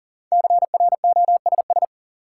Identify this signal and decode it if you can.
Morse code, heard as a keyed tone, about 37 wpm, 700 Hz. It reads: CROSS
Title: Sentences from Top 300 Words - New Word Drill - 40wpm